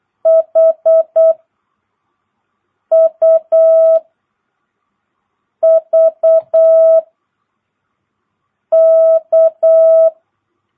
Morse code, heard as a keyed tone, about 8 wpm, 660 Hz. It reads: H U V K
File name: jh8cbh / 2022cw